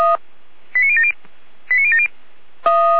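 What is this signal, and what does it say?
Five-tone selective call (ZVEI), heard as quick runs of five tones, tones of about 80 ms. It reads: 89070 89070
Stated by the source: ZVEI five-tone selective call received by SDR.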